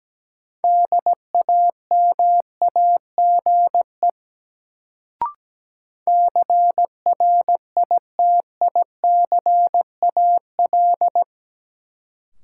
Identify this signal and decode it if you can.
Morse code, heard as a keyed tone, about 17 wpm, 700 Hz. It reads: DAMAGE CRITICAL
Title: Sentences from Top 1100 Words - New Word Drill - 17wpm